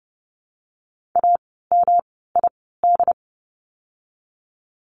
Morse code, heard as a keyed tone, about 30 wpm, spaced as for 17 wpm, 700 Hz. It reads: AMID
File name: course / 30:10wpm / 108 Lesson 18f - Letter m - Words 30wpm